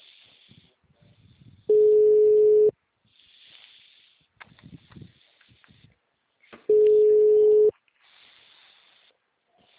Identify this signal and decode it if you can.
Ringback tone (Europe, 425 Hz).